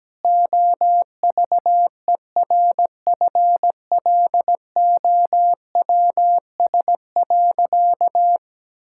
Morse code, heard as a keyed tone, about 17 wpm, 700 Hz. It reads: OVERFLOWS.
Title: Pangram Challenge v2 - 17wpm